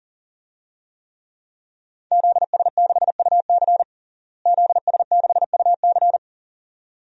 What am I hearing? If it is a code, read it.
Morse code, heard as a keyed tone, about 37 wpm, 700 Hz. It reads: ZS6UC ZS6UC
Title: DXCC-100 Call Signs - 5 Characters - Repeated 2x - 40wpm